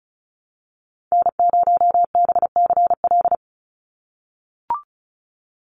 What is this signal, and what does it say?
Morse code, heard as a keyed tone, about 34 wpm, 700 Hz. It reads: N0BCL
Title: USA Call Signs 35wpm